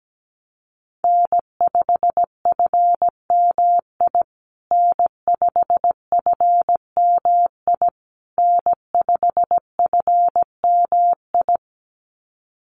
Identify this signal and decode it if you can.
Morse code, heard as a keyed tone, about 17 wpm, 700 Hz. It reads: N5FMI N5FMI N5FMI
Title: USA Call Signs Repeated 3x 17wpm